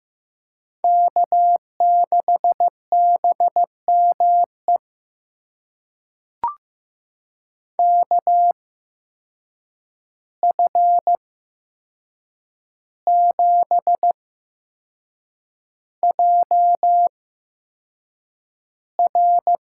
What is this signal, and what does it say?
Morse code, heard as a keyed tone, about 15 wpm, spaced as for 6 wpm, 700 Hz. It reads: K6BME KF7JR
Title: USA Call Signs - 8x Character Spacing - 15wpm